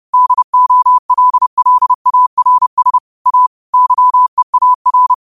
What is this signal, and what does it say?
Morse code, heard as a keyed tone, about 30 wpm, 1000 Hz. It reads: DOLLARS A YEAR
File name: qotdfiles / 30wpm